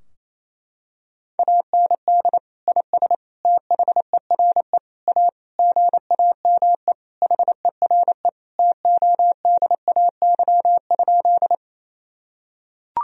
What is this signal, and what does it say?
Morse code, heard as a keyed tone, about 28 wpm, 700 Hz. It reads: AND IS THERE A GAME HERE TODAY?
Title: Sentences from Top 1000 Words - Review Part 3 - 28wpm